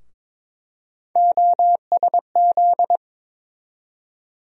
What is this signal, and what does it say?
Morse code, heard as a keyed tone, about 21 wpm, 700 Hz. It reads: OSZ
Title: USA Call Sign Postfixes - 3 Letters 22wpm